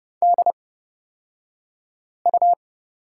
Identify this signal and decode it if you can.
Morse code, heard as a keyed tone, about 29 wpm, spaced as for 5 wpm, 700 Hz. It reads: DU
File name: Lesson 38d - Letter q - Two Characters 30wpm